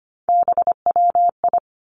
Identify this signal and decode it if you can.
Morse code, heard as a keyed tone, about 25 wpm, 700 Hz. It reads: BWI
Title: Three Challenging Letter-Number Combinations 25wpm